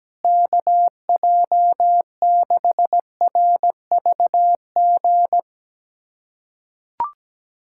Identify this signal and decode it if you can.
Morse code, heard as a keyed tone, about 17 wpm, 700 Hz. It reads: KJ6RVG
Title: USA Callsign DE Callsign - 6 Characters - 17wpm